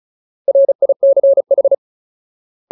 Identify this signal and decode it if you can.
Morse code, heard as a keyed tone, about 34 wpm, 550 Hz. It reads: RICH
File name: Ultimate Word List / 35WPM Ultimate